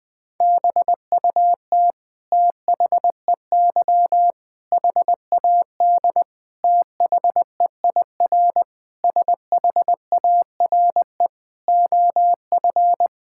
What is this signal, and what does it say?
Morse code, heard as a keyed tone, about 20 wpm, 700 Hz. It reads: BUT THEY HAD THEIR SHARE OF
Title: Sentences from Top 1000 Words - Review Part 6 - 20wpm